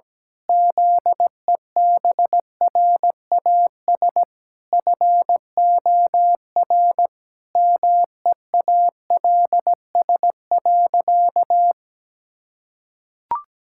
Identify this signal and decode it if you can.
Morse code, heard as a keyed tone, about 17 wpm, 700 Hz. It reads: ZEBRAS FOR MEALS.
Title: Pangram Challenge v2 - 17wpm